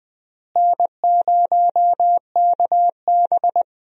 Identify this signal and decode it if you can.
Morse code, heard as a keyed tone, about 21 wpm, 700 Hz. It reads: N0KB
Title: USA Call Signs - 4 Characters - Repeated 3x - 20wpm